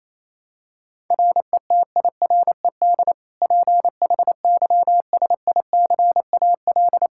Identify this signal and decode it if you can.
Morse code, heard as a keyed tone, about 27 wpm, 700 Hz. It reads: RETIRED PHYSICAL
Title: K1USN SST QSO Practice 28wpm